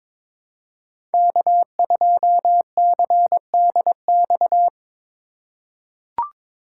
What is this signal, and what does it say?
Morse code, heard as a keyed tone, about 22 wpm, 700 Hz. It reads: K2CDX
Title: K1USN SST QSO Practice - 22wpm